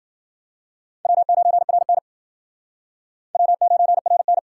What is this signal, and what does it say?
Morse code, heard as a keyed tone, about 60 wpm, 700 Hz. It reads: W9RN W9RN